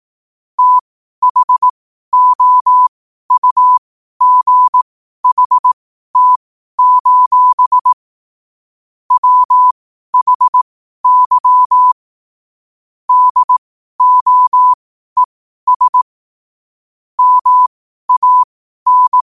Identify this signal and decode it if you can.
Morse code, heard as a keyed tone, about 18 wpm, spaced as for 12 wpm, 1000 Hz. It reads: THOUGHT: WHY DOES MAN
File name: qotdfiles / 12wpm